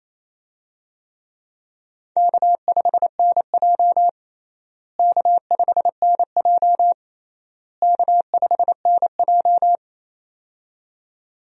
Morse code, heard as a keyed tone, about 28 wpm, 700 Hz. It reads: K5NJ K5NJ K5NJ
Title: USA Call Signs - 4 Characters - Repeated 3x - 28wpm